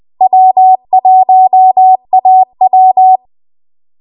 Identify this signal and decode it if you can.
Morse code, heard as a keyed tone, about 20 wpm, 750 Hz. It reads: W1AW